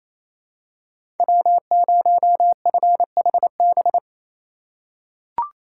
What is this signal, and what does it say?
Morse code, heard as a keyed tone, about 28 wpm, 700 Hz. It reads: W0FHB
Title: USA Call Signs - 5 Characters - 2x Character Spacing - 28wpm